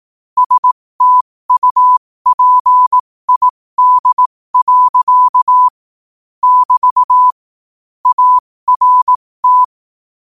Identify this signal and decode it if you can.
Morse code, heard as a keyed tone, about 18 wpm, spaced as for 15 wpm, 1000 Hz. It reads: STUPID. <BT> ART
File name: qotdfiles / 15wpm